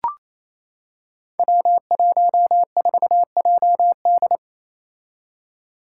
Morse code, heard as a keyed tone, about 28 wpm, 700 Hz. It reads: W14JD
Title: Five Challenging Letter-Number Combinations 28wpm